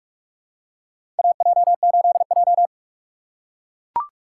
Morse code, heard as a keyed tone, about 42 wpm, 700 Hz. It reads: AJ8J
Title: USA Call Signs - 4 Characters - 2x Character Spacing - 45wpm